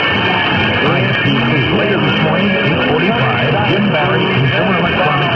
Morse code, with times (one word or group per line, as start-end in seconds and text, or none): none